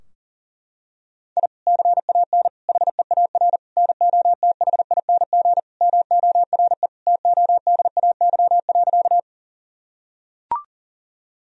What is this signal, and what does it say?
Morse code, heard as a keyed tone, about 40 wpm, 700 Hz. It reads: I CAN HEAR NOTHING MORE TODAY.
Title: Sentences from Top 1000 Words - Review Part 4 - 40wpm